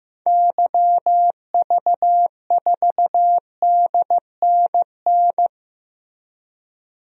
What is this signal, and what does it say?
Morse code, heard as a keyed tone, about 15 wpm, 700 Hz. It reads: YV4DNN
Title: DXCC-100 Call Signs - 6 Characters - 15wpm